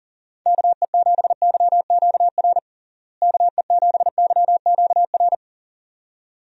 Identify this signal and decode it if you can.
Morse code, heard as a keyed tone, about 40 wpm, 700 Hz. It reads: KE7YQR KE7YQR
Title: USA Call Signs - 6 Characters - Repeated 2x - 40wpm